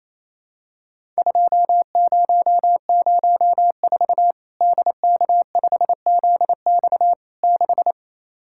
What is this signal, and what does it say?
Morse code, heard as a keyed tone, about 28 wpm, 700 Hz. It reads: 2004 DK5ZX 6
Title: Worked All Europe DX Contest - QTC Exchange - 28wpm